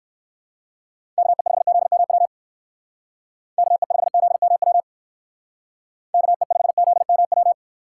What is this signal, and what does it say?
Morse code, heard as a keyed tone, about 59 wpm, 700 Hz. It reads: KE5ZMW KE5ZMW KE5ZMW